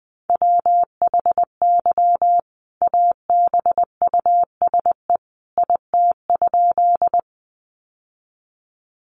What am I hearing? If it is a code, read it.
Morse code, heard as a keyed tone, about 20 wpm, 700 Hz. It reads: WHY ABUSE IT?
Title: Sentences from Top 1300 Words 20wpm